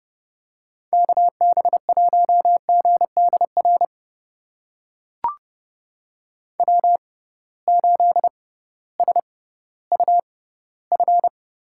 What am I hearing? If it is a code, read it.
Morse code, heard as a keyed tone, about 30 wpm, spaced as for 16 wpm, 700 Hz. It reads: KB1GDR W8SUF
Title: USA Call Signs - 6x Character Spacing - 30wpm